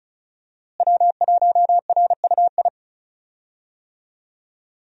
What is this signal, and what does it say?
Morse code, heard as a keyed tone, about 34 wpm, 700 Hz. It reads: W1RUI
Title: USA Call Signs - 5 Characters - Repeated 2x - 35wpm